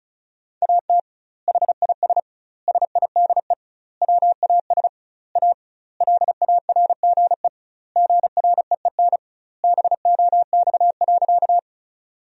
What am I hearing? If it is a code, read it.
Morse code, heard as a keyed tone, about 34 wpm, 700 Hz. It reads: AT HIS SIDE WAS A LARGE GREEN BOX.
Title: 2x Word-Spacing - Sentences from Top 500 Words - 35wpm